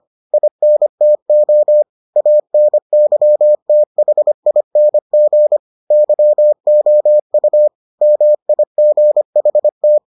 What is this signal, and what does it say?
Morse code, heard as a keyed tone, about 25 wpm, 600 Hz. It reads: INTO ANYTHING YOU MIGHT